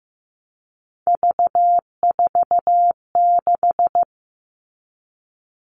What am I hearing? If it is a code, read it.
Morse code, heard as a keyed tone, about 15 wpm, 700 Hz. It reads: V46
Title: All Cut Numbers - 3-Digit 15wpm